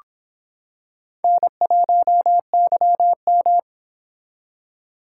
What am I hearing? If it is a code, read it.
Morse code, heard as a keyed tone, about 26 wpm, 700 Hz. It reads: N1YM